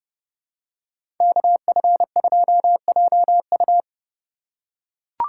Morse code, heard as a keyed tone, about 30 wpm, 700 Hz. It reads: KF2JU
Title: USA Call Signs - 5 Characters - 2x Character Spacing - 30wpm